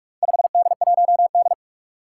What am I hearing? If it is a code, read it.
Morse code, heard as a keyed tone, about 46 wpm, 700 Hz. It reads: 5D1D
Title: Four Challenging Letter-Number Combinations 45wpm